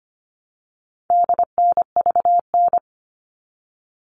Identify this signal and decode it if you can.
Morse code, heard as a keyed tone, about 25 wpm, 700 Hz. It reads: DNVN
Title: All Cut Numbers - 4-Digit 25wpm